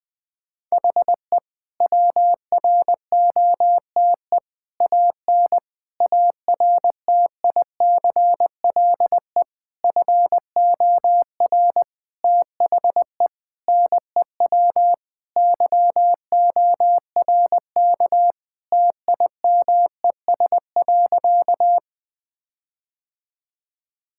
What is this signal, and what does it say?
Morse code, heard as a keyed tone, about 21 wpm, 700 Hz. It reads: HE WROTE AN ARTICLE FOR THE NEW YORK TIMES.
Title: Sentences with 5-Word Phrases - 20wpm